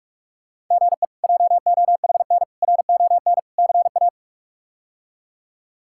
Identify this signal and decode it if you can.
Morse code, heard as a keyed tone, about 42 wpm, 700 Hz. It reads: GE JOHN RON CA
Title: K1USN SST QSO Practice - 45wpm